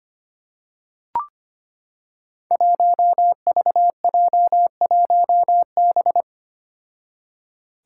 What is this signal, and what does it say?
Morse code, heard as a keyed tone, about 25 wpm, 700 Hz. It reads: 1VJ1B
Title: Five Challenging Letter-Number Combinations 25wpm